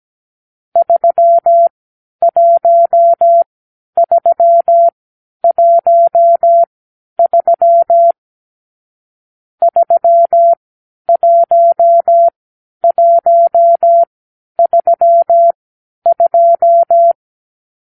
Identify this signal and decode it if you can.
Morse code, heard as a keyed tone, about 17 wpm, spaced as for 10 wpm, 680 Hz. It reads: 31313 31132
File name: k7qo / Lesson 44 (1-3)